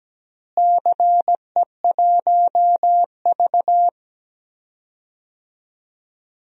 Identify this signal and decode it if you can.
Morse code, heard as a keyed tone, about 17 wpm, 700 Hz. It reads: CE1V
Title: DXCC-100 Call Signs - 4 Characters - Repeated 2x - 17wpm